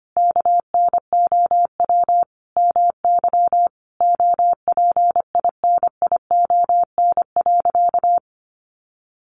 Morse code, heard as a keyed tone, about 25 wpm, 700 Hz. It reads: KNOW MY OPINION.
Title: Sentences from Top 1200 Words 25wpm